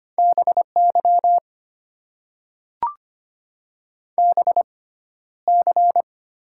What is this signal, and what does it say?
Morse code, heard as a keyed tone, about 25 wpm, spaced as for 13 wpm, 700 Hz. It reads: BY BC
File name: Top 90 QSO Elements - 6x Character Spacing - 25wpm